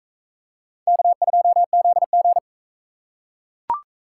Morse code, heard as a keyed tone, about 39 wpm, 700 Hz. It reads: K2ZG